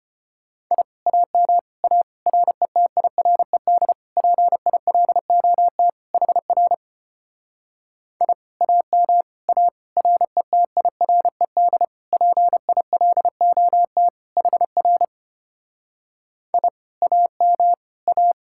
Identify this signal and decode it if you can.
Morse code, heard as a keyed tone, about 33 wpm, 700 Hz. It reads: I AM A RETIRED PILOT HR I AM A RETIRED PILOT HR I AM A